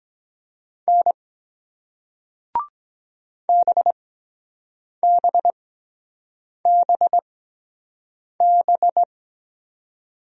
Morse code, heard as a keyed tone, about 21 wpm, spaced as for 7 wpm, 700 Hz. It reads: N BBBB
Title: Speed-Racing Single Letters 17wpm